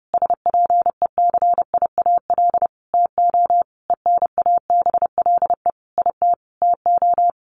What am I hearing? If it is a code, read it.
Morse code, heard as a keyed tone, about 30 wpm, 700 Hz. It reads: SPECIAL TO ENABLE IT TO
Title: Sentences from Top 1500 Words 30wpm